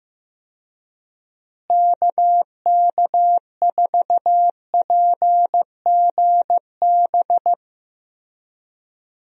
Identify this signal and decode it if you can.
Morse code, heard as a keyed tone, about 15 wpm, 700 Hz. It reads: KK4PGB
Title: USA Call Signs - 6 Characters - Repeated 2x - 15wpm